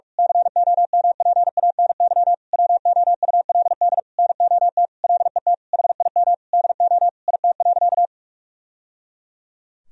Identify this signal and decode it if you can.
Morse code, heard as a keyed tone, about 46 wpm, 700 Hz. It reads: COMPANY WOULD NOT LET HIM DO IT.
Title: Sentences from Top 1000 Words - Review Part 3 - 45wpm